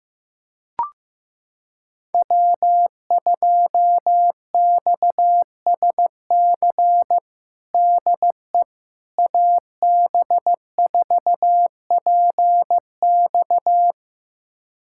Morse code, heard as a keyed tone, about 16 wpm, 700 Hz. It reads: W2XSC DE AB4PX
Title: USA Callsign DE Callsign - 5 Characters - 15wpm